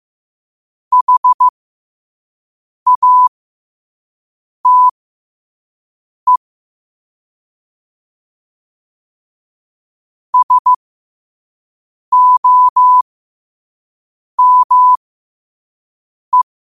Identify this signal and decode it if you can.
Morse code, heard as a keyed tone, about 15 wpm, spaced as for 5 wpm, 1000 Hz. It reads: HATE SOME
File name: qotdfiles / 5wpm